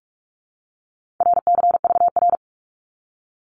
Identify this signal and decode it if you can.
Morse code, heard as a keyed tone, about 46 wpm, 700 Hz. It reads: RCVR
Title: Abbreviations 45wpm